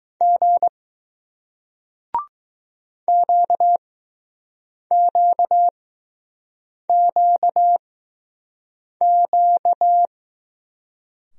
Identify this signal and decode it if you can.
Morse code, heard as a keyed tone, about 20 wpm, spaced as for 6 wpm, 700 Hz. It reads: G QQQQ